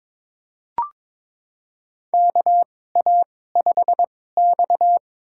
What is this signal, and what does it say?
Morse code, heard as a keyed tone, about 23 wpm, spaced as for 16 wpm, 700 Hz. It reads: KA5X
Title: USA Call Signs - 4 Characters - 2x Character Spacing - 22wpm